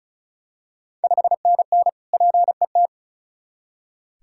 Morse code, heard as a keyed tone, about 36 wpm, 700 Hz. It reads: 5NN PET
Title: Ontario QSO Party Exchange 35wpm